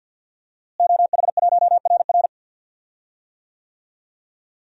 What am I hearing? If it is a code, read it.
Morse code, heard as a keyed tone, about 49 wpm, 700 Hz. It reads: OH1RR
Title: DXCC-100 Call Signs - 5 Characters - Repeated 3x - 50wpm